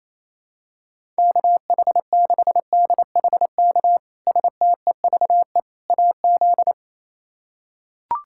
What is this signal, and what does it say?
Morse code, heard as a keyed tone, about 27 wpm, 700 Hz. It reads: KH6DHK STEVE AZ
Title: K1USN SST QSO Practice - 28wpm